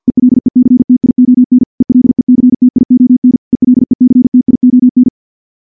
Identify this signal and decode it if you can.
Morse code, heard as a keyed tone, about 50 wpm, 250 Hz. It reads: ELECTION ELECTION ELECTION
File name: morse / morse-training-50WPM